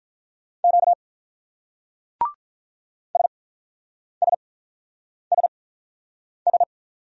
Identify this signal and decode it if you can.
Morse code, heard as a keyed tone, about 42 wpm, spaced as for 8 wpm, 700 Hz. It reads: Q SSSS
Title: Speed-Racing Single Letters 35wpm